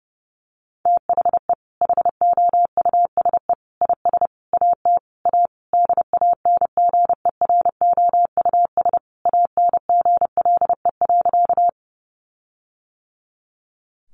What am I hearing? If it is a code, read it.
Morse code, heard as a keyed tone, about 30 wpm, 700 Hz. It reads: THE HOUSE IS AT A DANGEROUS ANGLE.
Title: Sentences from Top 1500 Words 30wpm